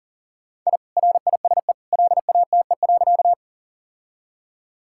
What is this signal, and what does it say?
Morse code, heard as a keyed tone, about 37 wpm, 700 Hz. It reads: I RISE LATE.
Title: Sentences from Top 1000 Words - Review Part 6 - 40wpm